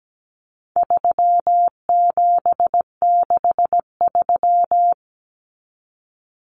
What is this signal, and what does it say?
Morse code, heard as a keyed tone, about 17 wpm, 700 Hz. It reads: 3763
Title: Common Cut Numbers - 4-Digit 17wpm